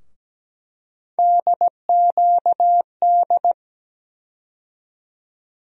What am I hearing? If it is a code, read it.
Morse code, heard as a keyed tone, about 17 wpm, 700 Hz. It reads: DQD
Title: USA Call Sign Postfixes - 3 Letters 17wpm